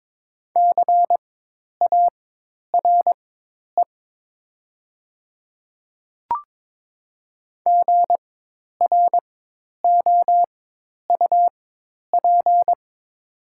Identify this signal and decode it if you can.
Morse code, heard as a keyed tone, about 22 wpm, spaced as for 10 wpm, 690 Hz. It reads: CARE GROUP
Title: 4x Character Spacing - Top 300 Words - 22wpm